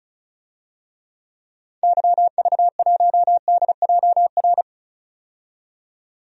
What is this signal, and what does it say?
Morse code, heard as a keyed tone, about 36 wpm, 700 Hz. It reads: YV1DJR
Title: DXCC-100 Call Signs - 6 Characters - 35wpm